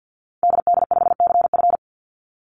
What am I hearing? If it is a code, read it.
Morse code, heard as a keyed tone, about 50 wpm, 700 Hz. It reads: DD5CF
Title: Call Signs - 50wpm